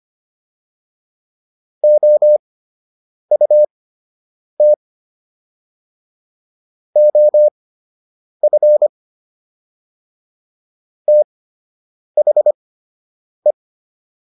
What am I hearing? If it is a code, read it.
Morse code, heard as a keyed tone, about 25 wpm, spaced as for 8 wpm, 600 Hz. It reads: OUT OF THE